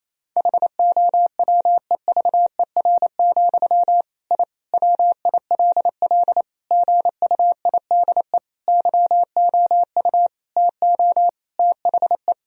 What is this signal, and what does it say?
Morse code, heard as a keyed tone, about 28 wpm, 700 Hz. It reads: HOWEVER, I WILL GUIDE YOU TO THE
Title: Sentences from Top 1000 Words - Review Part 3 - 28wpm